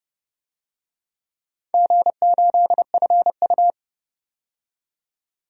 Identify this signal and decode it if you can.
Morse code, heard as a keyed tone, about 31 wpm, 700 Hz. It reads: G8FU
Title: DXCC-100 Call Signs - 4 Characters - 30wpm